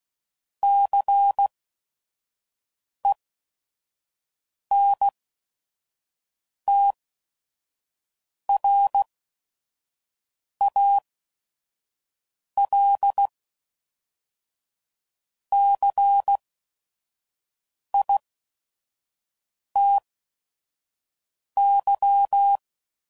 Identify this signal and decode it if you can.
Morse code, heard as a keyed tone, about 16 wpm, spaced as for 5 wpm, 800 Hz. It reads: CENTRALCITY
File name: CW Code Groups @5WPM / fromSM's / Cities